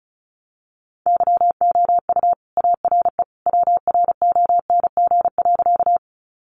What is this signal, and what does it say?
Morse code, heard as a keyed tone, about 34 wpm, 700 Hz. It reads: YOU ARE WRONG.
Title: Sentences from Top 900 Words 35wpm